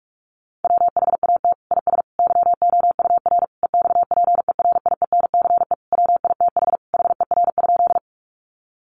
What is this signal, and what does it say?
Morse code, heard as a keyed tone, about 44 wpm, 700 Hz. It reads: WHAT IS YOUR EXPERIENCE WITH HER?
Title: Sentences from Top 600 Words 45wpm